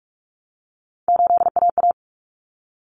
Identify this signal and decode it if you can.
Morse code, heard as a keyed tone, about 44 wpm, 700 Hz. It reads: 8AA
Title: Common Cut Numbers - 3-Digit 45wpm